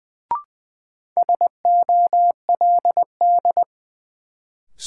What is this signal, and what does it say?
Morse code, heard as a keyed tone, about 20 wpm, 700 Hz. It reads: SOLD
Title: Sentences from Top 1400 Words - New Word Drill Alt - 20wpm